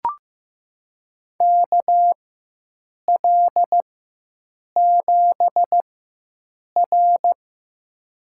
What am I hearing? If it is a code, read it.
Morse code, heard as a keyed tone, about 15 wpm, spaced as for 7 wpm, 700 Hz. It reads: KL7R
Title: USA Call Signs - 4 Characters - 4x Character Spacing - 15wpm